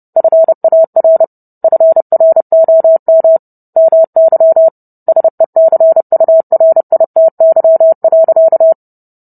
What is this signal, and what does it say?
Morse code, heard as a keyed tone, about 30 wpm, 650 Hz. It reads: FAR FROM MY SECURITY.